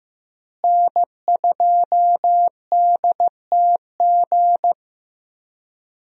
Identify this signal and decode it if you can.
Morse code, heard as a keyed tone, about 16 wpm, 700 Hz. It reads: N2DTG
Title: USA Call Signs - 5 Characters - Repeated 3x - 15wpm